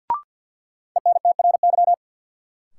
Morse code, heard as a keyed tone, about 50 wpm, 700 Hz. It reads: ENTRY
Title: Sentences from Top 1500 Words - New Word Drill - 50wpm